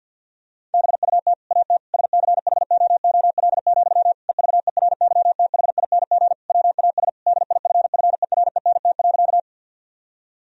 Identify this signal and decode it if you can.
Morse code, heard as a keyed tone, about 50 wpm, 700 Hz. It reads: BUT AT SCHOOL, EVERYTHING WAS DIFFERENT.
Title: Sentences from Top 1000 Words - Review Part 3 - 50wpm